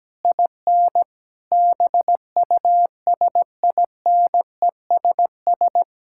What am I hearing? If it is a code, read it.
Morse code, heard as a keyed tone, about 17 wpm, 700 Hz. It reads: IN BUSINESS
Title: Sentences from Top 1000 Words - Review Part 6 - 17wpm